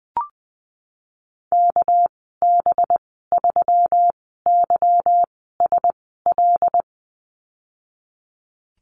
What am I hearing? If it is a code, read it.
Morse code, heard as a keyed tone, about 20 wpm, spaced as for 15 wpm, 700 Hz. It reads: KB3YSL
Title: 219 Lesson 35g - Number 0 - Call Signs 20wpm